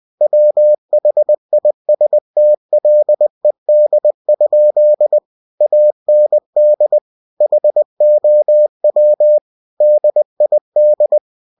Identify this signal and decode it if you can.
Morse code, heard as a keyed tone, about 20 wpm, 600 Hz. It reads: WHISTLED? AND HOW DID
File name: cw-books / winnie-the-pooh / winnie-20wpm-page-0041